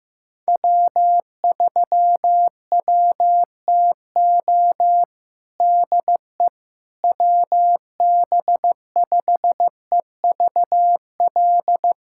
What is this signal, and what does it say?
Morse code, heard as a keyed tone, about 16 wpm, 700 Hz. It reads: W3WTO DE WB5EVL
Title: USA Callsign DE Callsign 15wpm